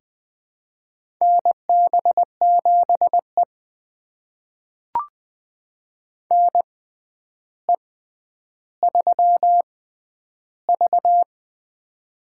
Morse code, heard as a keyed tone, about 20 wpm, spaced as for 10 wpm, 700 Hz. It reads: NB7E NE3V
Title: USA Call Signs - 4 Characters - 6x Character Spacing - 20wpm